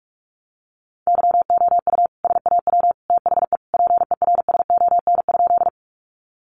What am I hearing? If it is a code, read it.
Morse code, heard as a keyed tone, about 44 wpm, 700 Hz. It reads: YOU SAW THE PERSON?
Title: Sentences from Top 400 Words 45wpm